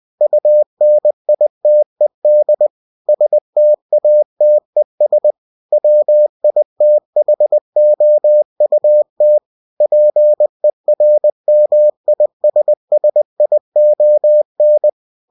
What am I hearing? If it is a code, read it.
Morse code, heard as a keyed tone, about 20 wpm, 600 Hz. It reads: UNITED STATES WITHOUT PERMISSION
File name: cw-books / winnie-the-pooh / winnie-20wpm-page-0112